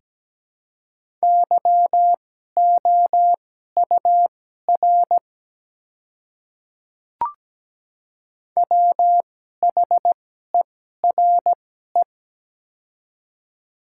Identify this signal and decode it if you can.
Morse code, heard as a keyed tone, about 17 wpm, spaced as for 12 wpm, 700 Hz. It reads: YOUR WHERE
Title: 2x Character Spacing - Top 200 Words - 17wpm